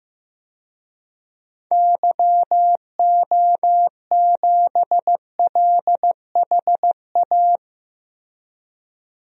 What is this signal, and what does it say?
Morse code, heard as a keyed tone, about 15 wpm, 700 Hz. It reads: YO7LHA
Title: DXCC-100 Call Signs - 6 Characters - Repeated 3x - 15wpm